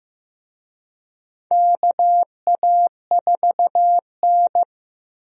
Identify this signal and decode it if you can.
Morse code, heard as a keyed tone, about 15 wpm, 700 Hz. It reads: KA4N
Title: USA Call Signs - 4 Characters - Repeated 2x - 15wpm